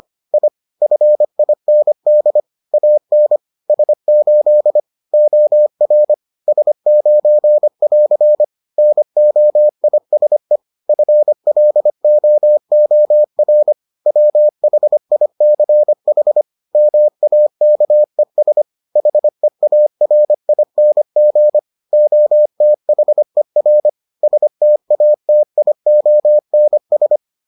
Morse code, heard as a keyed tone, about 25 wpm, 600 Hz. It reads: I FIND AN S8 OR S9<AR> NOISE FLOOR WHICH MAKES HEARING OTHER STATIONS